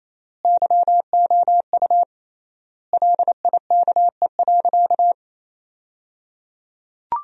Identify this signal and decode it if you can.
Morse code, heard as a keyed tone, about 28 wpm, 700 Hz. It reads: YOU LIKE.
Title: 3x Word-Spacing - Sentences from Top 500 Words - 28wpm